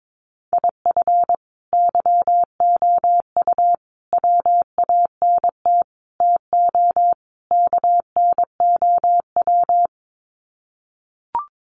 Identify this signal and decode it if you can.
Morse code, heard as a keyed tone, about 22 wpm, 700 Hz. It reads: IF YOU WANT TO KNOW
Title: Sets of 5 Words Encore2 22wpm